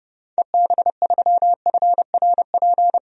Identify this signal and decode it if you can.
Morse code, heard as a keyed tone, about 30 wpm, 700 Hz. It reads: EB3FRP
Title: DXCC-100 Call Signs - 6 Characters - Repeated 2x - 30wpm